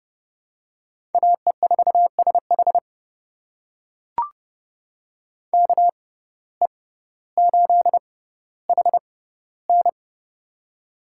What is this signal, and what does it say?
Morse code, heard as a keyed tone, about 30 wpm, spaced as for 15 wpm, 700 Hz. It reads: AE4SH KE8HN
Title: USA Call Signs - 5 Characters - 6x Character Spacing - 30wpm